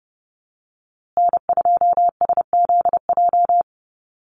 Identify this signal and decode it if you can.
Morse code, heard as a keyed tone, about 30 wpm, 700 Hz. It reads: N2SZJ
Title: USA Call Signs Repeated 2x 30wpm